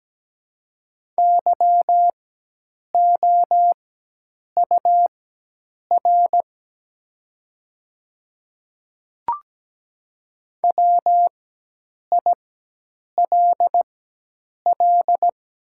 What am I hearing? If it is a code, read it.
Morse code, heard as a keyed tone, about 17 wpm, spaced as for 8 wpm, 700 Hz. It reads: YOUR WILL
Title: 4x Character Spacing - Top 100 Words - 17wpm